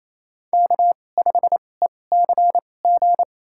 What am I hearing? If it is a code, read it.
Morse code, heard as a keyed tone, about 28 wpm, spaced as for 20 wpm, 700 Hz. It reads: K5ECG
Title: USA Call Signs - 2x Character Spacing - 28wpm